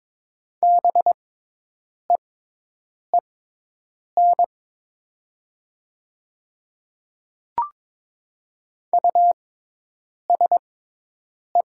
Morse code, heard as a keyed tone, about 22 wpm, spaced as for 8 wpm, 700 Hz. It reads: BEEN USE
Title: 6x Character Spacing - Top 100 Words - 22wpm